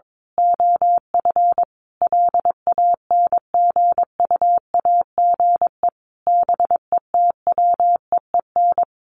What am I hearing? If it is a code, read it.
Morse code, heard as a keyed tone, about 22 wpm, 700 Hz. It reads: OF LANGUAGE BETWEEN
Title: Sentences from Top 500 Words 22wpm